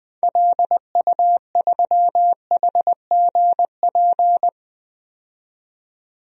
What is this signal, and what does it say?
Morse code, heard as a keyed tone, about 20 wpm, 700 Hz. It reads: LU3HGP
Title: DXCC-100 Call Signs - 6 Characters - 20wpm